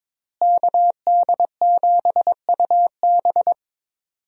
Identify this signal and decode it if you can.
Morse code, heard as a keyed tone, about 22 wpm, 700 Hz. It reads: KD7UB
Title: USA Call Signs - 5 Characters - Repeated 2x - 22wpm